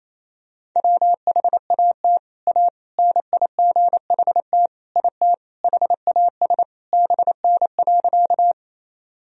Morse code, heard as a keyed tone, about 28 wpm, 700 Hz. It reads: WHAT A NIGHT IT HAS BN.
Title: Sentences from Top 300 Words and Abbreviations 28wpm